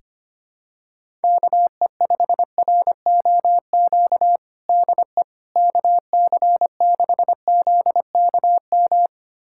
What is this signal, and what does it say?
Morse code, heard as a keyed tone, about 25 wpm, 700 Hz. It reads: KE5ROQ DE KC6ZKM
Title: USA Callsign DE Callsign - 6 Characters - 25wpm